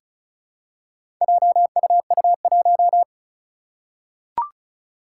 Morse code, heard as a keyed tone, about 36 wpm, 690 Hz. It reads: JUU1
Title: Four Challenging Letter-Number Combinations 35wpm